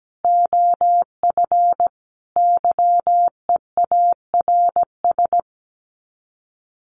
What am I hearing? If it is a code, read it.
Morse code, heard as a keyed tone, about 17 wpm, 700 Hz. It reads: OF YEARS
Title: Sets of 5 Words Encore 17wpm